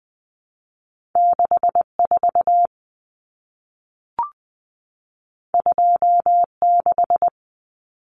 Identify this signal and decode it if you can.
Morse code, heard as a keyed tone, about 20 wpm, 700 Hz. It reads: 64 26
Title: Common Cut Numbers - 2-Digit 20wpm